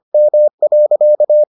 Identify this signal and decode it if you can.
Morse code, heard as a keyed tone, about 25 wpm, 600 Hz. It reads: M.